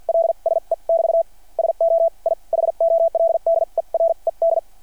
Morse code, heard as a keyed tone, about 52 wpm, spaced as for 41 wpm, 660 Hz. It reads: LSE <BT> SO I HOLDERED
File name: archived / CRSnet